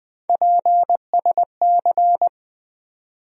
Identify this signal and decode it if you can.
Morse code, heard as a keyed tone, about 20 wpm, 700 Hz. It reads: PSC
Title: USA Call Sign Postfixes - 3 Letters 20wpm